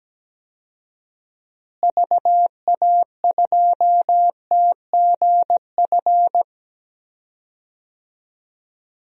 Morse code, heard as a keyed tone, about 17 wpm, 700 Hz. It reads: VA2TGF
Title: DXCC-100 Call Signs - 6 Characters - Repeated 3x - 17wpm